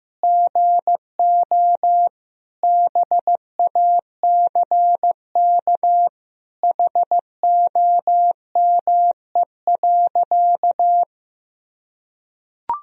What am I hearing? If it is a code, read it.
Morse code, heard as a keyed tone, about 15 wpm, 700 Hz. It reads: GO BACK HOME.
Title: Sentences from Top 500 Words and Abbreviations 15wpm